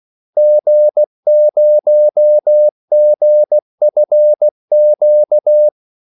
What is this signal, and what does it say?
Morse code, heard as a keyed tone, about 16 wpm, 600 Hz. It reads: G0GFQ
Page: Kaldesignaler 16 wpm